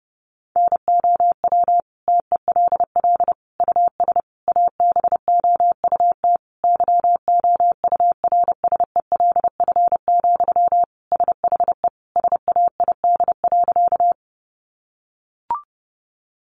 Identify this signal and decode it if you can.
Morse code, heard as a keyed tone, about 30 wpm, 700 Hz. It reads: NOW TELL US ABOUT YOURSELF, SHE SAID.
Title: Sentences from Top 600 Words 30wpm